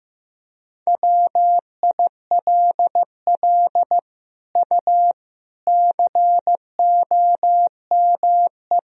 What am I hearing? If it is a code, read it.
Morse code, heard as a keyed tone, about 16 wpm, 700 Hz. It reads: WILL U COME
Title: Sentences from Top 100 Words and Abbreviations 15wpm